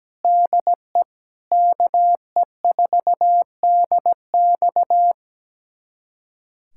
Morse code, heard as a keyed tone, about 17 wpm, 700 Hz. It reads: DE KE4DX
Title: USA Callsign DE Callsign 17wpm